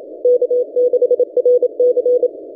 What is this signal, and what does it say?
Morse code, heard as a keyed tone, about 27 wpm, 490 Hz. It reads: K6RC
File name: ft5zm-17cw-2014-01-27-0120z-lp